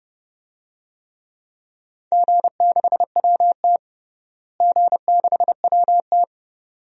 Morse code, heard as a keyed tone, about 30 wpm, 700 Hz. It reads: G6WT G6WT
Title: DXCC-100 Call Signs - 4 Characters - Repeated 2x - 30wpm